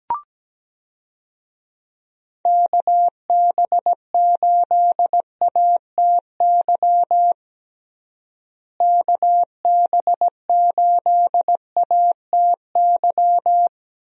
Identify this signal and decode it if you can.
Morse code, heard as a keyed tone, about 17 wpm, 700 Hz. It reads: KB8ATY KB8ATY
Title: USA Call Signs - 6 Characters - Repeated 2x - 17wpm